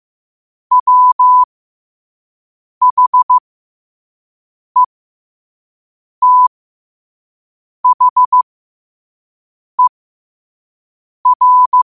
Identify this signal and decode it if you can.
Morse code, heard as a keyed tone, about 15 wpm, spaced as for 5 wpm, 1000 Hz. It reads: WHETHER
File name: qotdfiles / 5wpm